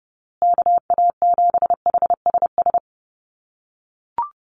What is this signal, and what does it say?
Morse code, heard as a keyed tone, about 30 wpm, 700 Hz. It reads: KA7HSS
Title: USA Call Signs Repeated 2x 30wpm